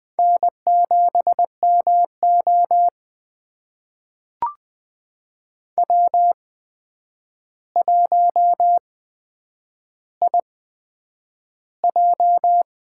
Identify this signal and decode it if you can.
Morse code, heard as a keyed tone, about 20 wpm, spaced as for 9 wpm, 700 Hz. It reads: N7MO W1IJ
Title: USA Call Signs - 4 Characters - 8x Character Spacing - 20wpm